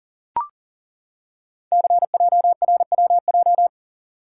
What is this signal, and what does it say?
Morse code, heard as a keyed tone, about 39 wpm, 700 Hz. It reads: CJRWJ
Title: Code Groups - 5 Letters - 40wpm